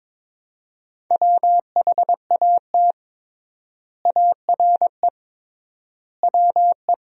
Morse code, heard as a keyed tone, about 23 wpm, 700 Hz. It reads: WHAT ARE WE
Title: 3x Word-Spacing - Sentences from Top 100 Words - 22wpm